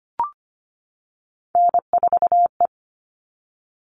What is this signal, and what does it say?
Morse code, heard as a keyed tone, about 25 wpm, 700 Hz. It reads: N4E
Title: All Cut Numbers - 3-Digit 25wpm